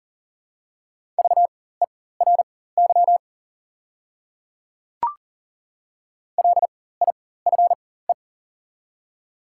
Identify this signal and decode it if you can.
Morse code, heard as a keyed tone, about 40 wpm, spaced as for 19 wpm, 700 Hz. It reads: VERY LIFE
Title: 4x Character Spacing - Top 300 Words - 40wpm